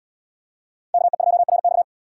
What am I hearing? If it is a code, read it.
Morse code, heard as a keyed tone, about 71 wpm, 700 Hz. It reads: KE2RX